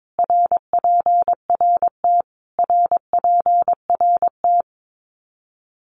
Morse code, heard as a keyed tone, about 22 wpm, 700 Hz. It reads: RPRT RPRT
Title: Abbreviations Repeated 3x - 22wpm